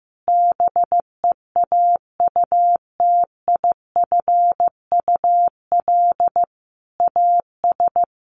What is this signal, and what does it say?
Morse code, heard as a keyed tone, about 14 wpm, 700 Hz. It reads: BEAUTIFUL AS
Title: Sentences from Top 1500 Words 15wpm